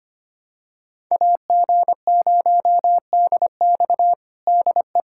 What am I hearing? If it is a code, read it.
Morse code, heard as a keyed tone, about 25 wpm, 700 Hz. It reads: AG0DX DE
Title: USA Callsign DE Callsign - 5 Characters - 25wpm